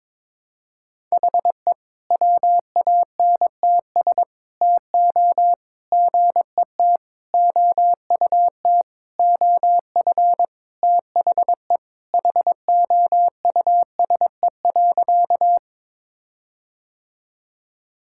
Morse code, heard as a keyed tone, about 22 wpm, 700 Hz. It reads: HE WANTS TO GET OUT OF THE HOUSE.
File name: practice / 22wpm / Sentences with 4-Word Phrases - 22wpm